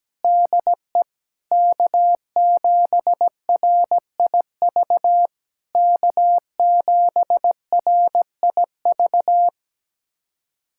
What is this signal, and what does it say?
Morse code, heard as a keyed tone, about 17 wpm, 700 Hz. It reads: DE K7RIV K7RIV
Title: SOTA QSO Practice 17wpm